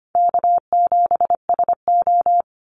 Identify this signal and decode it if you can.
Morse code, heard as a keyed tone, about 25 wpm, 690 Hz. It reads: K7SO
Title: NA SOTA Activator Callsigns 25wpm